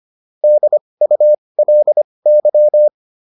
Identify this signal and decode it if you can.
Morse code, heard as a keyed tone, about 25 wpm, spaced as for 20 wpm, 600 Hz. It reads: DULY